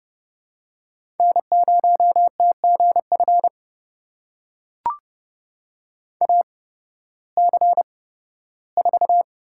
Morse code, heard as a keyed tone, about 30 wpm, spaced as for 16 wpm, 700 Hz. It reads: N0TGF AC4
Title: USA Call Signs - 5 Characters - 8x Character Spacing - 30wpm